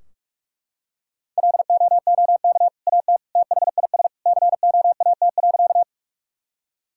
Morse code, heard as a keyed tone, about 41 wpm, 700 Hz. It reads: LOOK AT THIS COAT.
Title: Sentences from Top 1600 Words 45wpm